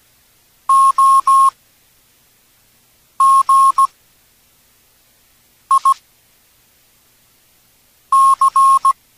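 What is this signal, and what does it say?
Morse code, heard as a keyed tone, about 17 wpm, spaced as for 4 wpm, 1080 Hz. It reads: OGIC
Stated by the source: Morse from an experimental text-to-speech project for radio amateurs.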